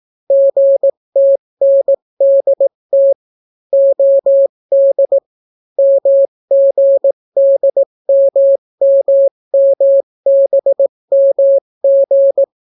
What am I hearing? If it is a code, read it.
Morse code, heard as a keyed tone, about 18 wpm, 550 Hz. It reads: GTNDT OD MGDMMMBMG